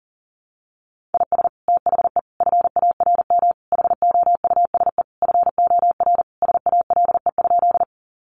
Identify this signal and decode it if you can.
Morse code, heard as a keyed tone, about 40 wpm, 700 Hz. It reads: IS THE FARM HOUSE FOR SALE?
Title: Sentences from Top 1400 Words 40wpm